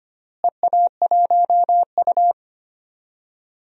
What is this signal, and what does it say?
Morse code, heard as a keyed tone, about 25 wpm, 700 Hz. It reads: EA1U
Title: DXCC-100 Call Signs - 4 Characters - Repeated 2x - 25wpm